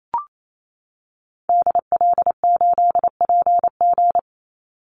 Morse code, heard as a keyed tone, about 28 wpm, 700 Hz. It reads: DL8PG
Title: Call Signs - 28wpm